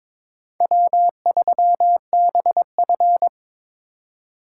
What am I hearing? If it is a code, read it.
Morse code, heard as a keyed tone, about 26 wpm, 700 Hz. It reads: W3BF
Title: Speed Racing - USA 1x2 Call Signs 22wpm